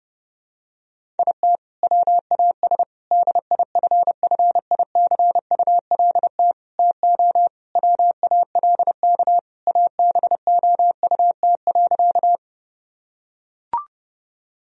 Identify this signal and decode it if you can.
Morse code, heard as a keyed tone, about 30 wpm, 700 Hz. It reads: IT WAS DIFFICULT TO WALK ABOUT.
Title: Sentences from Top 1000 Words - Review Part 1 - 30wpm